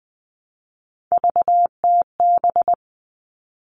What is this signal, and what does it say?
Morse code, heard as a keyed tone, about 20 wpm, 700 Hz. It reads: VTB
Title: All Cut Numbers - 3-Digit 20wpm